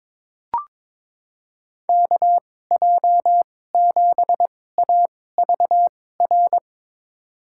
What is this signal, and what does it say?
Morse code, heard as a keyed tone, about 21 wpm, spaced as for 16 wpm, 700 Hz. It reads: KJ7AVR
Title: USA Call Signs - 2x Character Spacing - 22wpm